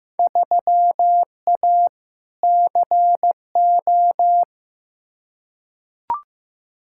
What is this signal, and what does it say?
Morse code, heard as a keyed tone, about 15 wpm, 700 Hz. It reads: 3A CO
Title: Field Day Exchanges Encore - 15wpm